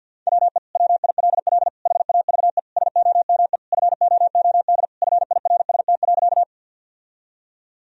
Morse code, heard as a keyed tone, about 52 wpm, 700 Hz. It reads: WE WILL HAVE SOME FOOD FIRST.
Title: Sentences from Top 1000 Words - Review Part 1 - 50wpm